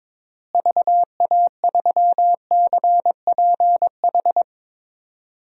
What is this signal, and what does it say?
Morse code, heard as a keyed tone, about 21 wpm, 700 Hz. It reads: VA3CPH
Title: DXCC-100 Call Signs - 6 Characters - Repeated 3x - 22wpm